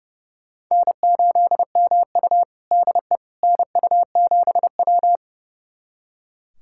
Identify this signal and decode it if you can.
Morse code, heard as a keyed tone, about 30 wpm, 700 Hz. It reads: N8MU DE NU7W
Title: USA Callsign DE Callsign - 4 Characters - 30wpm